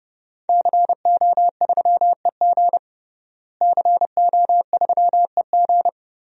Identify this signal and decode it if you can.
Morse code, heard as a keyed tone, about 31 wpm, 700 Hz. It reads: CO3EG CO3EG
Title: DXCC-100 Call Signs - 5 Characters - Repeated 2x - 30wpm